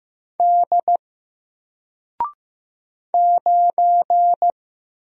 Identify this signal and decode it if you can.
Morse code, heard as a keyed tone, about 15 wpm, 700 Hz. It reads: D 9
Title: Single Letter-Number 15wpm